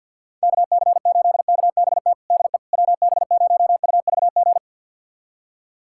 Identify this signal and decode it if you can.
Morse code, heard as a keyed tone, about 49 wpm, 700 Hz. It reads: KC8KBT DE WB0UVG
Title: USA Callsign DE Callsign - 6 Characters - 50wpm